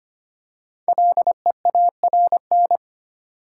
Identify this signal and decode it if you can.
Morse code, heard as a keyed tone, about 25 wpm, 700 Hz. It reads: LEARN
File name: 5 Letter Words - First 3 Letters Given - 25wpm